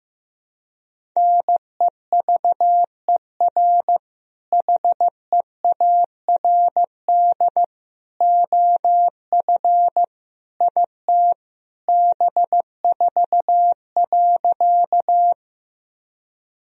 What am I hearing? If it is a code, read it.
Morse code, heard as a keyed tone, about 16 wpm, 700 Hz. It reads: NEVER HEARD OF IT B4.
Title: Sentences from Top 400 Words and Abbreviations 15wpm